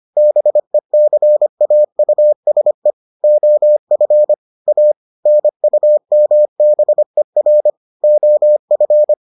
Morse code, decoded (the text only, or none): BECAUSE OF A NUMBER OF